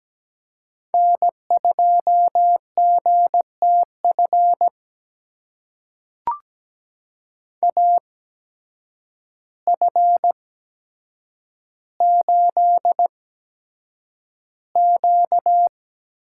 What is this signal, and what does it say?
Morse code, heard as a keyed tone, about 17 wpm, spaced as for 8 wpm, 700 Hz. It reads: N2GTF AF8Q
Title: USA Call Signs - 8x Character Spacing - 17wpm